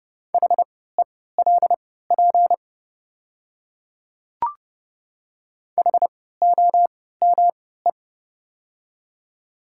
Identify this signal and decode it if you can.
Morse code, heard as a keyed tone, about 29 wpm, spaced as for 17 wpm, 700 Hz. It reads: HELP HOME
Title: 3x Character Spacing - Top 300 Words - 30wpm